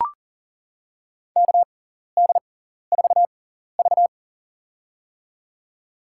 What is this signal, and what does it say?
Morse code, heard as a keyed tone, about 43 wpm, spaced as for 14 wpm, 700 Hz. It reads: KD4V
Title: USA Call Signs - 4 Characters - 6x Character Spacing - 40wpm